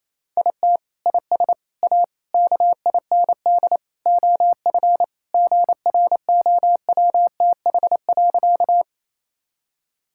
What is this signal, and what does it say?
Morse code, heard as a keyed tone, about 28 wpm, 700 Hz. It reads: IT IS A KIND OF GROWTH.
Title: Sentences from Top 1000 Words - Review Part 5 - 28wpm